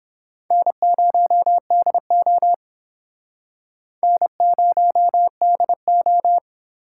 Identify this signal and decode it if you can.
Morse code, heard as a keyed tone, about 25 wpm, 700 Hz. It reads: N0DO N0DO
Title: Speed Racing - USA 1x2 Call Signs 22wpm